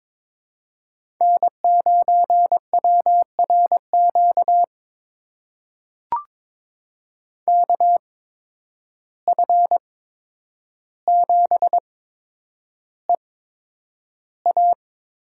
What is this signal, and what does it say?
Morse code, heard as a keyed tone, about 22 wpm, spaced as for 9 wpm, 700 Hz. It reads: N9WRQ KF7EA
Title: USA Call Signs - 8x Character Spacing - 22wpm